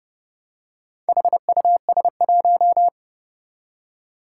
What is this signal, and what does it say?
Morse code, heard as a keyed tone, about 31 wpm, 700 Hz. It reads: HUS1
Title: Four Challenging Letter-Number Combinations 30wpm